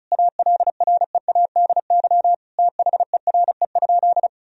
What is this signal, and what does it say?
Morse code, heard as a keyed tone, about 36 wpm, 700 Hz. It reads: ALREADY THERE?
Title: Sentences from Top 1000 Words - Review Part 4 - 35wpm